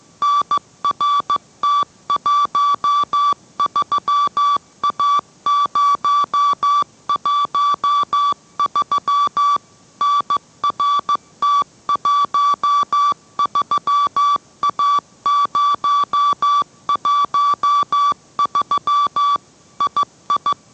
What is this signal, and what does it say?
Morse code, heard as a keyed tone, about 15 wpm, 1160 Hz. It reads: NRT13A013 NRT13A013 II